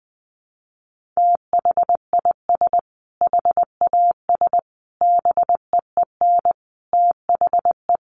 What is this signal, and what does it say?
Morse code, heard as a keyed tone, about 19 wpm, 700 Hz. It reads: THIS HAS BEEN THE
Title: Sentences from Top 1400 Words 20wpm